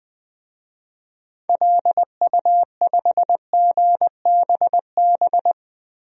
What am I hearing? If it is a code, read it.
Morse code, heard as a keyed tone, about 21 wpm, 700 Hz. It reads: LU5GBB
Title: DXCC-100 Call Signs - 6 Characters - 20wpm